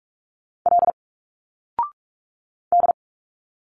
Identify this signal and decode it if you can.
Morse code, heard as a keyed tone, about 46 wpm, 700 Hz. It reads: L D
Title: Single Letters 45wpm